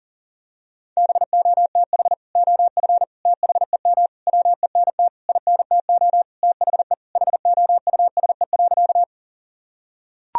Morse code, heard as a keyed tone, about 40 wpm, 700 Hz. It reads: BOTH OF THEM WENT INTO THE HOUSE.
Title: Sentences from Top 1000 Words - Review Part 1 - 40wpm